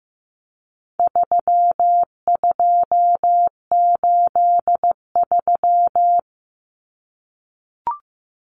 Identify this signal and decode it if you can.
Morse code, heard as a keyed tone, about 15 wpm, 700 Hz. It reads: 3283
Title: Common Cut Numbers - 4-Digit 15wpm